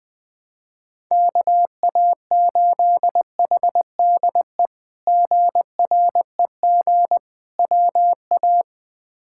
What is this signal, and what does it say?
Morse code, heard as a keyed tone, about 20 wpm, 700 Hz. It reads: KA8HDE GREG WA
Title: K1USN SST QSO Practice - 20wpm